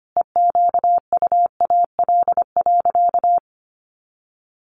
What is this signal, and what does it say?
Morse code, heard as a keyed tone, about 25 wpm, 700 Hz. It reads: EQUAL.
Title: Sentences from Top 1500 Words 25wpm